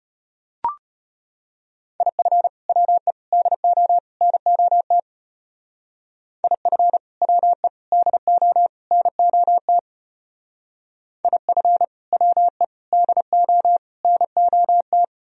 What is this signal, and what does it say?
Morse code, heard as a keyed tone, about 33 wpm, 700 Hz. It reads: IF WE DO NOT IF WE DO NOT IF WE DO NOT